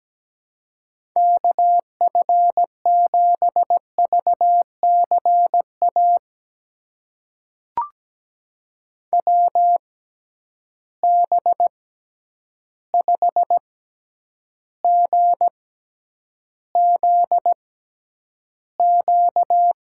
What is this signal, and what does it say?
Morse code, heard as a keyed tone, about 17 wpm, spaced as for 9 wpm, 700 Hz. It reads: KF7VCA WB5GZQ
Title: USA Call Signs - 6 Characters - 6x Character Spacing - 17wpm